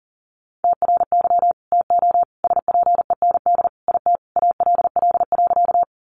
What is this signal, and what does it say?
Morse code, heard as a keyed tone, about 40 wpm, 700 Hz. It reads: TRY TO SPEND IT ALL.
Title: Sentences from Top 800 Words 40wpm